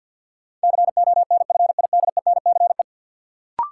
Morse code, heard as a keyed tone, about 50 wpm, 700 Hz. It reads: CONFIDENCE
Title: Sentences from Top 1500 Words - New Word Drill Alt - 50wpm